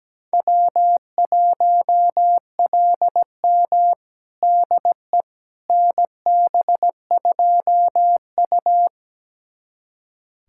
Morse code, heard as a keyed tone, about 17 wpm, 700 Hz. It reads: W1LM DE NB2U
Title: USA Callsign DE Callsign - 4 Characters - 17wpm